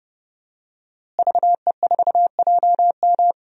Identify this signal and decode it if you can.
Morse code, heard as a keyed tone, about 32 wpm, 700 Hz. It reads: VE4JM
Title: DXCC-100 Call Signs - 5 Characters - Repeated 3x - 30wpm